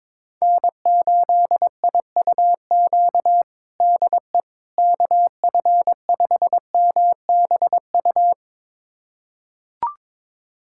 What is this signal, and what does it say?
Morse code, heard as a keyed tone, about 22 wpm, 700 Hz. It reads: N8IUQ DE KF5MBU
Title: USA Callsign DE Callsign 22wpm